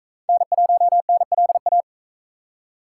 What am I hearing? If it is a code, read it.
Morse code, heard as a keyed tone, about 41 wpm, 700 Hz. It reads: N1NLA